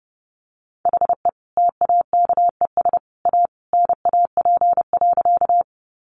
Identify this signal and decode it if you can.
Morse code, heard as a keyed tone, about 30 wpm, 700 Hz. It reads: HE TAKES A NAP.
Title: Easy Phrases 30wpm